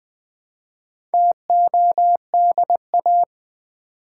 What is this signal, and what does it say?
Morse code, heard as a keyed tone, about 20 wpm, 700 Hz. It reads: TODA
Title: Build A Word - 1 Letter At A Time - 5-Letter Words - 20wpm